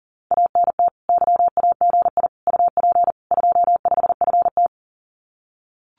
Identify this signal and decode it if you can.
Morse code, heard as a keyed tone, about 42 wpm, 700 Hz. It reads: ANT YAGI UP 25FT
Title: Common QSO Phrases Encore 40wpm